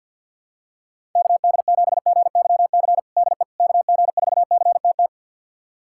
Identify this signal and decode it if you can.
Morse code, heard as a keyed tone, about 52 wpm, 700 Hz. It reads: KD7GYC DE KG4CTT
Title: USA Callsign DE Callsign - 6 Characters - 50wpm